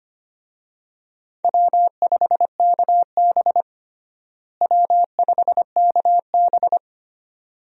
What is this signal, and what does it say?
Morse code, heard as a keyed tone, about 25 wpm, 700 Hz. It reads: W5KB W5KB
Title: USA Call Signs - 4 Characters - Repeated 2x - 25wpm